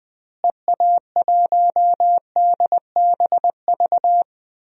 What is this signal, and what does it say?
Morse code, heard as a keyed tone, about 20 wpm, 700 Hz. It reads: EA1DBV
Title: DXCC-100 Call Signs - 6 Characters - Repeated 3x - 20wpm